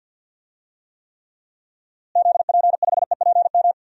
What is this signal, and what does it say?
Morse code, heard as a keyed tone, about 52 wpm, 700 Hz. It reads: ZP5EPM
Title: DXCC-100 Call Signs - 6 Characters - Repeated 2x - 50wpm